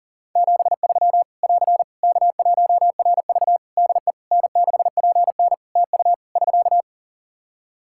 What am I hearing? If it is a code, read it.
Morse code, heard as a keyed tone, about 40 wpm, 700 Hz. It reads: 73 <AR> K1RV DE N6PN TU <SK>
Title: Practice QSO - Exchange 4 - Ending - 40wpm